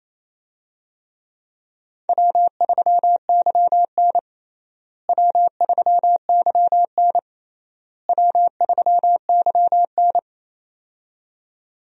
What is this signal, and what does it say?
Morse code, heard as a keyed tone, about 28 wpm, 700 Hz. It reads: W3YN W3YN W3YN
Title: USA Call Signs - 4 Characters - Repeated 3x - 28wpm